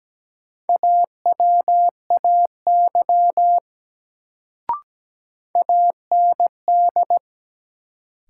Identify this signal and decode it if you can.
Morse code, heard as a keyed tone, about 17 wpm, 700 Hz. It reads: AWAY AND
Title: Sets of 2 Words - New Word Drill - 17wpm